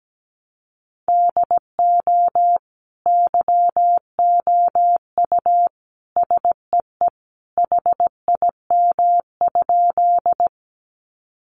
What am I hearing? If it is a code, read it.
Morse code, heard as a keyed tone, about 17 wpm, 700 Hz. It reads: DO YOU SEE HIM?
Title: Sentences from Top 100 Words 17wpm